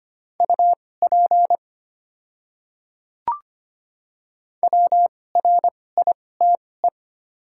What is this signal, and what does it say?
Morse code, heard as a keyed tone, about 25 wpm, spaced as for 18 wpm, 700 Hz. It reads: UP WRITE
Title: 2x Character Spacing - Top 100 Words - 25wpm